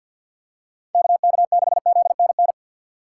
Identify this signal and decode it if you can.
Morse code, heard as a keyed tone, about 49 wpm, 700 Hz. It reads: KK6ZNN
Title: USA Call Signs - 6 Characters - 3x Character Spacing - 50wpm